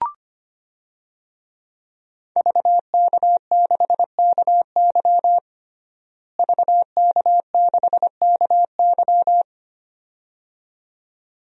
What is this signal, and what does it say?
Morse code, heard as a keyed tone, about 25 wpm, 700 Hz. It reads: VK6KY VK6KY
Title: DXCC-100 Call Signs - 5 Characters - Repeated 2x - 25wpm